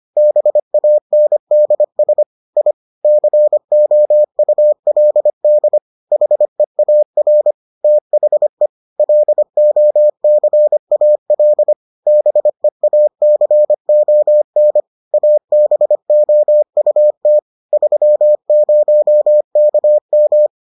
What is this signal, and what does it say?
Morse code, heard as a keyed tone, about 25 wpm, 600 Hz. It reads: BANDS I COULD HEAR THE LOCAL BEACON ABOUT 30KM